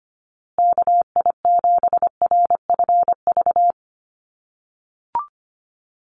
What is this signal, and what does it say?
Morse code, heard as a keyed tone, about 25 wpm, 700 Hz. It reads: KI7RFV
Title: USA Call Signs Repeated 2x 25wpm